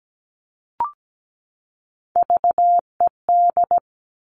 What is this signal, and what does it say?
Morse code, heard as a keyed tone, about 17 wpm, 700 Hz. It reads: VED
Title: All Cut Numbers - 3-Digit 17wpm